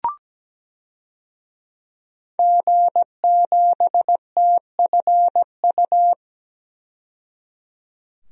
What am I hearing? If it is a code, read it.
Morse code, heard as a keyed tone, about 17 wpm, 700 Hz. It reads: G7TFU
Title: DXCC-100 Call Signs - 5 Characters - 17wpm